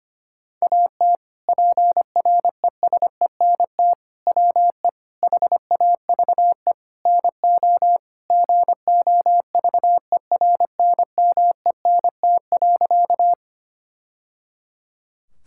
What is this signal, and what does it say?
Morse code, heard as a keyed tone, about 25 wpm, 700 Hz. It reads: AT PRESENT WE HAVE NO GOVERNMENT.
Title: Sentences from Top 1000 Words - Review Part 5 - 25wpm